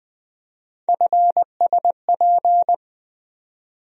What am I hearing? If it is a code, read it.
Morse code, heard as a keyed tone, about 20 wpm, 700 Hz. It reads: FSP
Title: USA Call Sign Postfixes - 3 Letters 20wpm